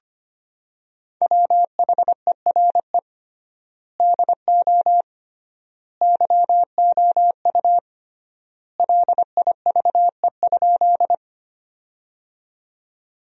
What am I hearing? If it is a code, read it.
Morse code, heard as a keyed tone, about 25 wpm, 700 Hz. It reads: WHERE DO YOU LIVE?
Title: 3x Word-Spacing - Sentences from Top 200 Words - 25wpm